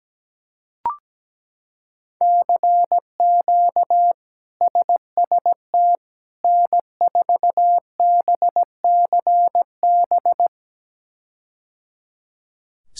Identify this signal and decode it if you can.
Morse code, heard as a keyed tone, about 17 wpm, 700 Hz. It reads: CQ SST N4BCB
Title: K1USN SST QSO Practice - 17wpm